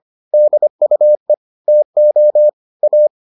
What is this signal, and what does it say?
Morse code, heard as a keyed tone, about 25 wpm, 600 Hz. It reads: DUE TO A